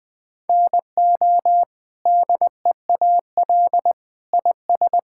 Morse code, heard as a keyed tone, about 20 wpm, 700 Hz. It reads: NO DEAL IS
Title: Sentences with 8-Word Phrases - 20wpm